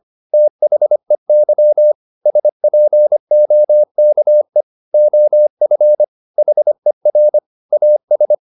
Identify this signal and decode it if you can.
Morse code, heard as a keyed tone, about 25 wpm, 600 Hz. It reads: THEY SPOKE OF HER AS
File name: the old man and the sea part 1 25wpm